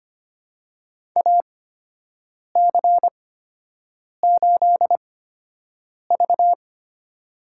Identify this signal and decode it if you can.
Morse code, heard as a keyed tone, about 25 wpm, spaced as for 7 wpm, 700 Hz. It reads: AC8V
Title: USA Call Signs - 4 Characters - 8x Character Spacing - 25wpm